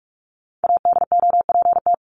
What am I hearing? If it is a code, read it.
Morse code, heard as a keyed tone, about 44 wpm, 700 Hz. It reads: ADOPT
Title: ED Suffix - Top 100 Words with Base Words 45wpm